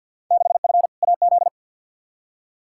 Morse code, heard as a keyed tone, about 47 wpm, 700 Hz. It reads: 6F AZ
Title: Field Day Exchanges Encore - 50wpm